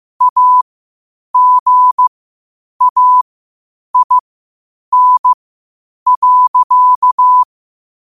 Morse code, heard as a keyed tone, about 15 wpm, spaced as for 8 wpm, 1000 Hz. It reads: AGAIN.
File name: 8wpm